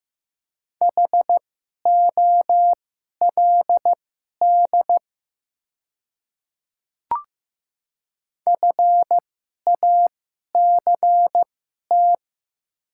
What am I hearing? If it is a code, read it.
Morse code, heard as a keyed tone, about 15 wpm, spaced as for 11 wpm, 700 Hz. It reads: HOLD FACT
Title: 2x Character Spacing - Top 500 Words - 15wpm